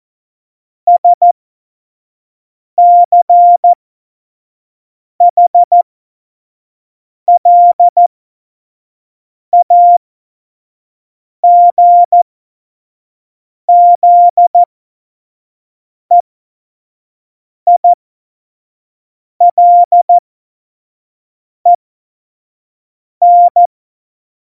Morse code, heard as a keyed tone, about 14 wpm, spaced as for 5 wpm, 700 Hz. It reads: SCHLAGZEILEN
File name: Morsekurs / Witze / Witz 20. Oktober 2020, 25 F70